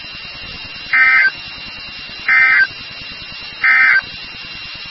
Emergency Alert System (SAME) tones, heard as harsh buzzing bursts. It is heard from 0.8 to 4.1 s.
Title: NOAA Weather Radio WXK36